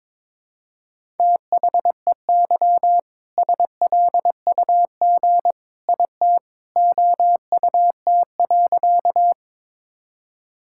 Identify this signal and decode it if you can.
Morse code, heard as a keyed tone, about 22 wpm, 700 Hz. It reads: THEY SLUG IT OUT.
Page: morse-code-ninja-practice-22wpm